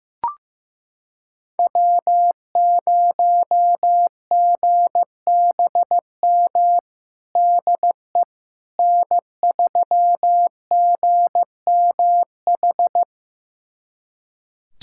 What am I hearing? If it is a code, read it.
Morse code, heard as a keyed tone, about 15 wpm, 700 Hz. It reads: W0GBM DE N3GMH
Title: USA Callsign DE Callsign - 5 Characters - 15wpm